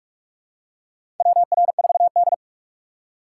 Morse code, heard as a keyed tone, about 46 wpm, 700 Hz. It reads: WR4D
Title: USA Call Signs - 4 Characters - 6x Character Spacing - 45wpm